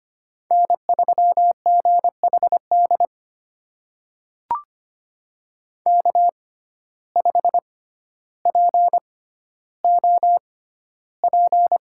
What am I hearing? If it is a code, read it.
Morse code, heard as a keyed tone, about 25 wpm, spaced as for 13 wpm, 700 Hz. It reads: N3GHD K5POP
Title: USA Call Signs - 5 Characters - 6x Character Spacing - 25wpm